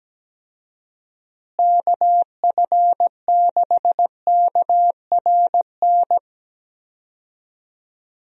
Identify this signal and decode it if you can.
Morse code, heard as a keyed tone, about 17 wpm, 700 Hz. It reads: KF6KRN
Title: USA Call Signs - 6 Characters - Repeated 2x - 17wpm